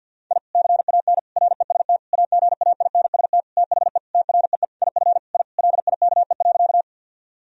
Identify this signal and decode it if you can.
Morse code, heard as a keyed tone, about 51 wpm, 700 Hz. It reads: I CAN REST AGAINST THE TREE IF I LIKE.
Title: Sentences from Top 1000 Words - Review Part 1 - 50wpm